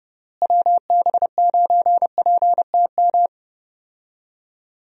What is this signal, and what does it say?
Morse code, heard as a keyed tone, about 30 wpm, 700 Hz. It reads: WB9PTM
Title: USA Call Signs - 6 Characters - Repeated 3x - 30wpm